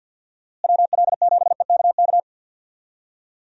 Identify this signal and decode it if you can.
Morse code, heard as a keyed tone, about 49 wpm, 700 Hz. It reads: WL7EKK
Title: USA Call Signs - 6 Characters - Repeated 3x - 50wpm